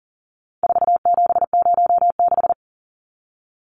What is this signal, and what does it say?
Morse code, heard as a keyed tone, about 40 wpm, 700 Hz. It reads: 4706
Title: Numbers - 4-Digit 40wpm